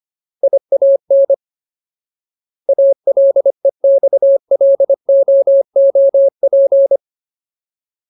Morse code, heard as a keyed tone, about 25 wpm, 550 Hz. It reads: IAN ALEXLOOP
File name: Ultimate Word List / 25WPM Ultimate